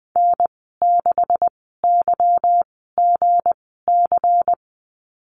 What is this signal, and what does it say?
Morse code, heard as a keyed tone, about 20 wpm, spaced as for 14 wpm, 700 Hz. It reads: N6YGC
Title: 161 Lesson 26g - Letter g - Call Signs 20wpm